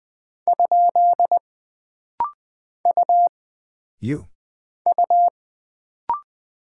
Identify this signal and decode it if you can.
Morse code, heard as a keyed tone, about 20 wpm, 700 Hz. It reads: ? U U
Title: Single Character (Letter:Number:Punctuation:Prosign) - Rapid-Fire - 20wpm